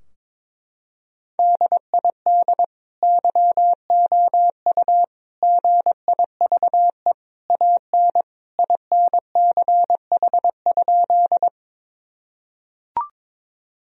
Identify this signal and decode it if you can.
Morse code, heard as a keyed tone, about 22 wpm, 700 Hz. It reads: DID YOU GIVE AN INCH?
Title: Sentences from Top 1000 Words - Review Part 3 - 22wpm